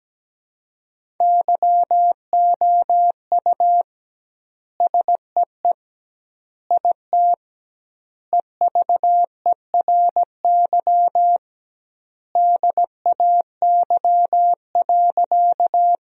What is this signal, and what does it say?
Morse code, heard as a keyed tone, about 17 wpm, 700 Hz. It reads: YOU SEE IT EVERY DAY.
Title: 2x Word-Spacing - Sentences from Top 200 Words - 17wpm